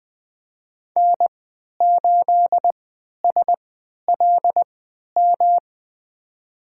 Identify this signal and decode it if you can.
Morse code, heard as a keyed tone, about 20 wpm, spaced as for 11 wpm, 700 Hz. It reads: N8SLM
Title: USA Call Signs - 3x Character Spacing - 20wpm